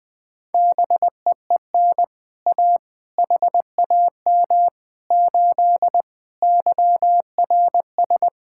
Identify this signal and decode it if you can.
Morse code, heard as a keyed tone, about 20 wpm, 700 Hz. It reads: BEEN A HAM 8 YRS
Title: Speed-Racing Common QSO Phrases - 20wpm